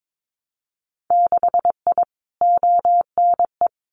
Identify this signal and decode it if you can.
Morse code, heard as a keyed tone, about 22 wpm, 700 Hz. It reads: 6I ONE
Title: Winter Field Day Exchanges Encore 22wpm